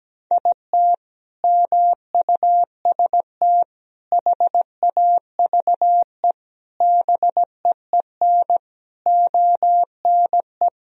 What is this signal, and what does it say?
Morse code, heard as a keyed tone, about 17 wpm, 700 Hz. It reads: IT MUST HAVE BEEN ONE
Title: Sentences from Top 1000 Words - Review Part 1 - 17wpm